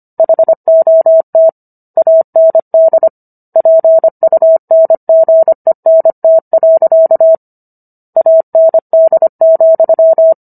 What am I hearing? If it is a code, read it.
Morse code, heard as a keyed tone, about 25 wpm, 650 Hz. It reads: HOT AND PUNGENT. AND,